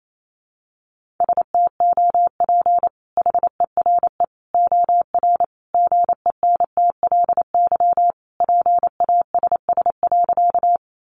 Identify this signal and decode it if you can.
Morse code, heard as a keyed tone, about 28 wpm, 700 Hz. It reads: STOP HERE OR GENTLY PASS.
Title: Sentences from Top 1500 Words 28wpm